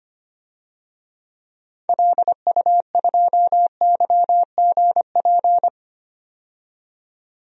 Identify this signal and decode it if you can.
Morse code, heard as a keyed tone, about 25 wpm, 700 Hz. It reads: LU2YGP
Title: DXCC-100 Call Signs - 6 Characters - 25wpm